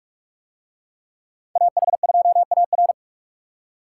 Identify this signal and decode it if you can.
Morse code, heard as a keyed tone, about 44 wpm, 700 Hz. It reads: AH2AR